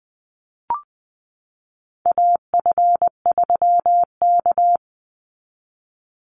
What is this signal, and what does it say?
Morse code, heard as a keyed tone, about 20 wpm, 700 Hz. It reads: AF3K
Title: Call Signs Repeated 3x - 20wpm